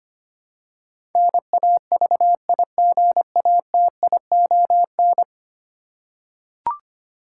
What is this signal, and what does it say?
Morse code, heard as a keyed tone, about 25 wpm, 690 Hz. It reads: NAVIGATION
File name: 10 Letter Words - First 3 Letters Given - 25wpm